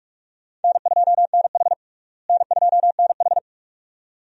Morse code, heard as a keyed tone, about 41 wpm, 700 Hz. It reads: N2NH N2NH
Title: USA Call Signs - 4 Characters - Repeated 2x - 45wpm